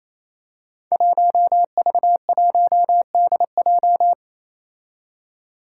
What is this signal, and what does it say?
Morse code, heard as a keyed tone, about 29 wpm, 700 Hz. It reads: 1V1DJ
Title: Five Challenging Letter-Number Combinations 28wpm